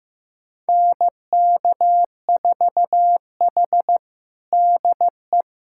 Morse code, heard as a keyed tone, about 15 wpm, 700 Hz. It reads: NK4H DE
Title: USA Callsign DE Callsign 15wpm